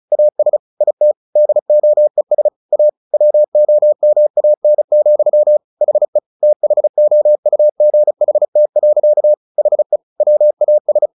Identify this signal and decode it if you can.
Morse code, heard as a keyed tone, about 34 wpm, 600 Hz. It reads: AS IT DOES A WOMAN, HE THOUGHT. HE WAS
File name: the old man and the sea / the old man and the sea 1 35wpm